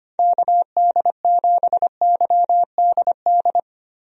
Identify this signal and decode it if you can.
Morse code, heard as a keyed tone, about 25 wpm, 700 Hz. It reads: KD7YDD
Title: USA Call Signs - 6 Characters - Repeated 2x - 25wpm